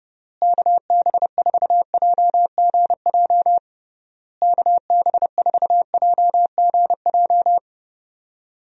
Morse code, heard as a keyed tone, about 28 wpm, 700 Hz. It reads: KB4JGJ KB4JGJ
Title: USA Call Signs - 6 Characters - Repeated 3x - 30wpm